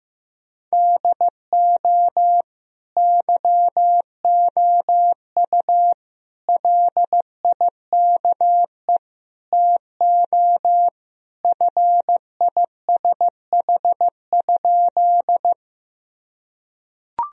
Morse code, heard as a keyed tone, about 15 wpm, 700 Hz. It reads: DO YOU LIKE TO FISH?
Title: Sentences from Top 1000 Words - Review Part 1 - 15wpm